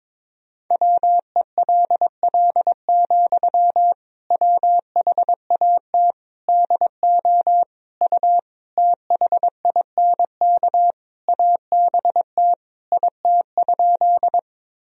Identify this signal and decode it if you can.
Morse code, heard as a keyed tone, about 22 wpm, 700 Hz. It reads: WELL, WHAT DO U THINK ABT IT?
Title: Sentences from Top 200 Words and Abbreviations 22wpm